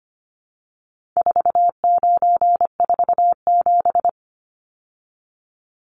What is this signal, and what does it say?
Morse code, heard as a keyed tone, about 25 wpm, 700 Hz. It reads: 4947
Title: Numbers - 4-Digit 25wpm